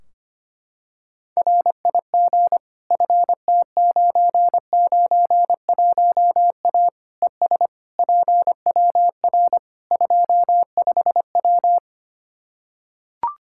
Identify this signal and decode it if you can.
Morse code, heard as a keyed tone, about 25 wpm, 700 Hz. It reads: RIG FT991A ES PWR 25W
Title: Practice QSO - Exchange 2 - Info - 25wpm